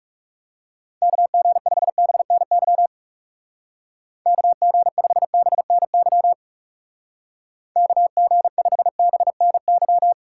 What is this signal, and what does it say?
Morse code, heard as a keyed tone, about 38 wpm, 700 Hz. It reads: KG5BNY KG5BNY KG5BNY